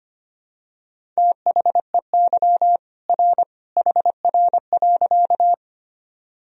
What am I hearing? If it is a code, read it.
Morse code, heard as a keyed tone, about 25 wpm, 700 Hz. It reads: THEY R HR.
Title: Sentences from Top 200 Words and Abbreviations 25wpm